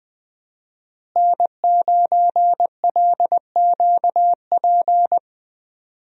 Morse code, heard as a keyed tone, about 20 wpm, 700 Hz. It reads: N9LQP
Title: USA Call Signs - 5 Characters - Repeated 3x - 20wpm